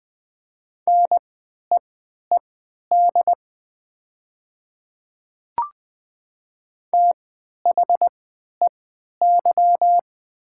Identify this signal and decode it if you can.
Morse code, heard as a keyed tone, about 20 wpm, spaced as for 11 wpm, 700 Hz. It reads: NEED THEY
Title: 3x Character Spacing - Top 300 Words - 20wpm